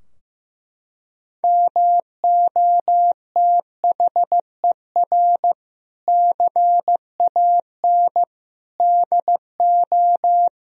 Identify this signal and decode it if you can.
Morse code, heard as a keyed tone, about 15 wpm, 700 Hz. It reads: MOTHER CAN DO
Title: Sentences from Top 1000 Words - Review Part 4 - 15wpm